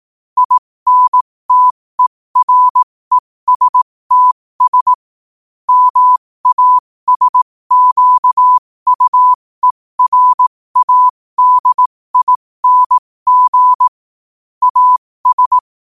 Morse code, heard as a keyed tone, about 18 wpm, spaced as for 15 wpm, 1000 Hz. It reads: INTERESTS MASQUERADING AS
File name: qotdfiles / 15wpm